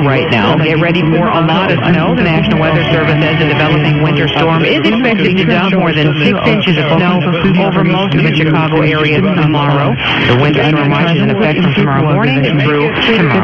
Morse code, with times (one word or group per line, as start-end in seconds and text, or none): none